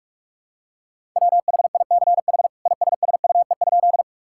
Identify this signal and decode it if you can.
Morse code, heard as a keyed tone, about 46 wpm, 700 Hz. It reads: WHICH ISSUE?
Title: Sentences from Top 1000 Words - Review Part 6 - 45wpm